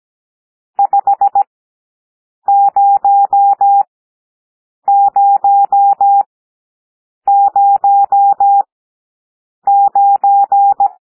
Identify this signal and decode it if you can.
Morse code, heard as a keyed tone, about 17 wpm, spaced as for 7 wpm, 800 Hz. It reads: 50009
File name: SOUND NUMBERS 17 wpm